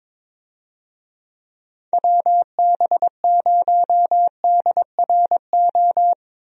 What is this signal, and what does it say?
Morse code, heard as a keyed tone, about 21 wpm, 700 Hz. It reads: WB0DRO
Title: USA Call Signs - 6 Characters - 22wpm